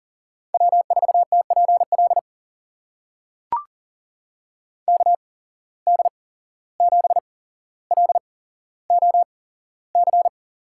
Morse code, heard as a keyed tone, about 40 wpm, spaced as for 16 wpm, 700 Hz. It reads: W4TPL KD7LOC
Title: USA Call Signs - 8x Character Spacing - 40wpm